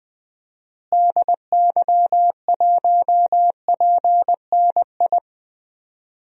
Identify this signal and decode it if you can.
Morse code, heard as a keyed tone, about 20 wpm, 700 Hz. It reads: DY1PNI
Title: DXCC-100 Call Signs - 6 Characters - Repeated 2x - 20wpm